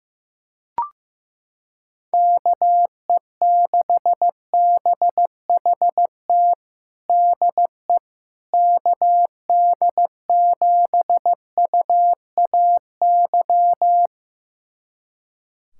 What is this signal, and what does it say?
Morse code, heard as a keyed tone, about 15 wpm, 700 Hz. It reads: KE6BHT DE KD7UAY
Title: USA Callsign DE Callsign 15wpm